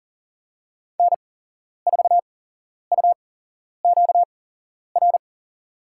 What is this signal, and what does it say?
Morse code, heard as a keyed tone, about 38 wpm, spaced as for 11 wpm, 700 Hz. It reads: N4UQR
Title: USA Call Signs - 5 Characters - 8x Character Spacing - 40wpm